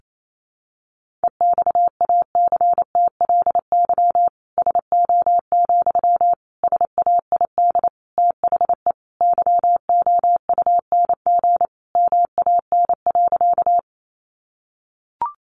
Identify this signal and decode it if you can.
Morse code, heard as a keyed tone, about 28 wpm, 700 Hz. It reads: EXACTLY SO, SAID THE YOUNG MAN.
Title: Sentences from Top 900 Words 28wpm